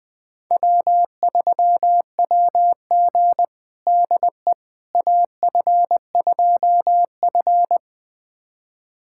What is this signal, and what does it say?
Morse code, heard as a keyed tone, about 20 wpm, 700 Hz. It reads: W3WG DE AF2F
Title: USA Callsign DE Callsign - 4 Characters - 20wpm